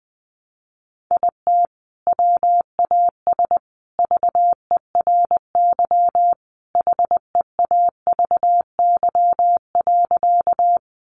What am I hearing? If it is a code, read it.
Morse code, heard as a keyed tone, about 19 wpm, 700 Hz. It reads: IT WAS VERY HEAVY.
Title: Sentences from Top 1100 Words 20wpm